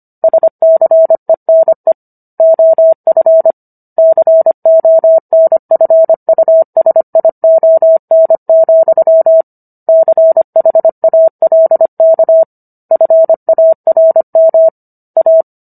Morse code, heard as a keyed tone, about 25 wpm, 650 Hz. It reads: SCENE OF CONFUSION, CHALK FARM A